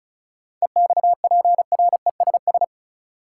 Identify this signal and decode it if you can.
Morse code, heard as a keyed tone, about 36 wpm, 700 Hz. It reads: EXPRESS
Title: Seven Letter Words - 35wpm